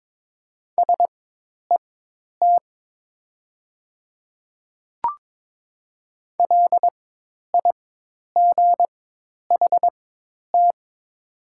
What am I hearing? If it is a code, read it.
Morse code, heard as a keyed tone, about 22 wpm, spaced as for 10 wpm, 700 Hz. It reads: SET LIGHT
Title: 4x Character Spacing - Top 300 Words - 22wpm